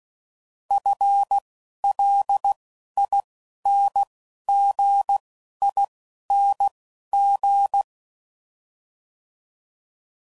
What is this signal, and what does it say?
Morse code, heard as a keyed tone, about 16 wpm, spaced as for 11 wpm, 800 Hz. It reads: FLINGING
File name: Prefixes / FLs & THs / FL Prefixes 1